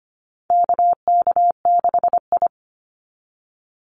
Morse code, heard as a keyed tone, about 25 wpm, 700 Hz. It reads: KK6I
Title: Call Signs Repeated 2x - 25wpm